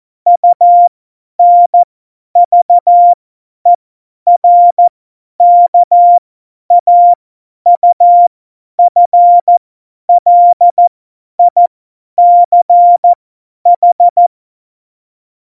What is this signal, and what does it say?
Morse code, heard as a keyed tone, about 14 wpm, spaced as for 10 wpm, 700 Hz. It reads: UNVERKAUFLICH
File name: Morsekurs / Witze / Witz 03. September 2019, 50 F70